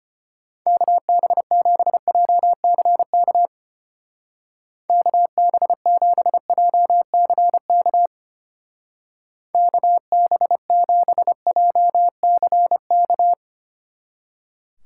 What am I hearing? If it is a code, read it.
Morse code, heard as a keyed tone, about 32 wpm, 700 Hz. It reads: KB7JCK KB7JCK KB7JCK